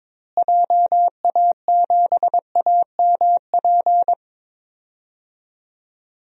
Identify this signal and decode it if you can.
Morse code, heard as a keyed tone, about 22 wpm, 700 Hz. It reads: JA7AMP
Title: DXCC-100 Call Signs - 6 Characters - Repeated 3x - 22wpm